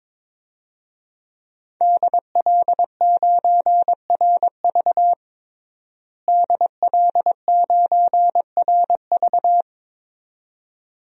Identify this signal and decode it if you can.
Morse code, heard as a keyed tone, about 22 wpm, 700 Hz. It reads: DL9RV DL9RV
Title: DXCC-100 Call Signs - 5 Characters - Repeated 2x - 22wpm